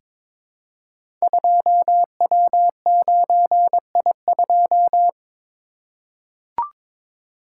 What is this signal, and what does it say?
Morse code, heard as a keyed tone, about 22 wpm, 700 Hz. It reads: 2W9I2
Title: Code Groups - 5 Letter:Numbers - 22wpm